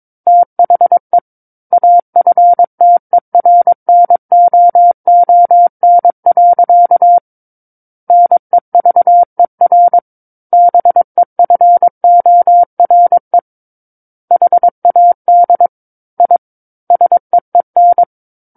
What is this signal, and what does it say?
Morse code, heard as a keyed tone, about 22 wpm, 700 Hz. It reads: THE AFTERNOON. NEVER BEFORE HAD I SEEN
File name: Book One / 13-Chapter 13 - 22 WPM